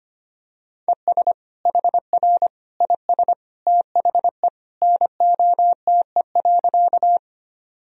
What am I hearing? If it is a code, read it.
Morse code, heard as a keyed tone, about 25 wpm, 700 Hz. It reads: ES HR IS THE NOTE.
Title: Sentences from Top 500 Words and Abbreviations 25wpm